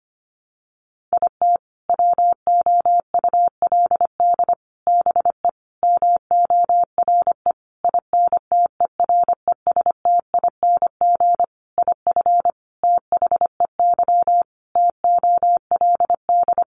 Morse code, heard as a keyed tone, about 25 wpm, 700 Hz. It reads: IT WOULD BE MORE INTERESTING IF THEY TOLD
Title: Sentences from Top 1000 Words 25wpm